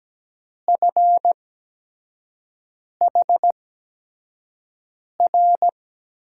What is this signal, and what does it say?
Morse code, heard as a keyed tone, about 17 wpm, spaced as for 5 wpm, 700 Hz. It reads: FHR
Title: USA Call Signs - 6 Characters - 8x Character Spacing - 17wpm